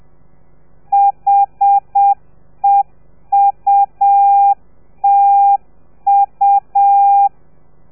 Morse code, heard as a keyed tone, about 7 wpm, 800 Hz. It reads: HEUTU